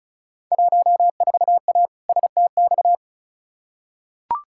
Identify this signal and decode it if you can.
Morse code, heard as a keyed tone, about 34 wpm, 700 Hz. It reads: 14A STX
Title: Field Day Exchanges Encore - 35wpm